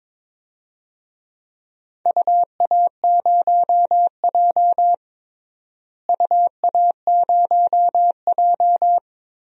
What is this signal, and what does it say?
Morse code, heard as a keyed tone, about 22 wpm, 700 Hz. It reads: UA0J UA0J
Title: DXCC-100 Call Signs - 4 Characters - Repeated 3x - 22wpm